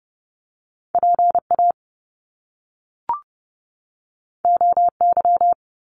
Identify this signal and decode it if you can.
Morse code, heard as a keyed tone, about 30 wpm, 700 Hz. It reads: PA OY
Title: Top Two Letter Combinations in Words 30wpm